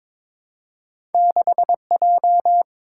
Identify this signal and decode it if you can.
Morse code, heard as a keyed tone, about 23 wpm, 700 Hz. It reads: 6J
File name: practice / 22wpm / Two Letter-Number Combinations - 22wpm